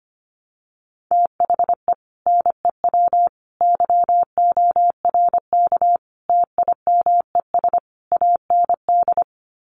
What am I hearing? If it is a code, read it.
Morse code, heard as a keyed tone, about 25 wpm, 700 Hz. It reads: THE NEW YORK TIMES AND
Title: Sets of 5 Words Encore 25wpm